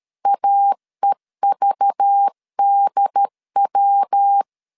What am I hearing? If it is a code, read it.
Morse code, heard as a keyed tone, about 13 wpm, 800 Hz. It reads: AEVDW
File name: LEZIONI / 7 GF9Z / 7 G 12wpm 2055